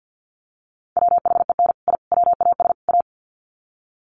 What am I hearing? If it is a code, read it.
Morse code, heard as a keyed tone, about 50 wpm, 700 Hz. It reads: WHEN I WAS A
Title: Sets of 4 Words 50wpm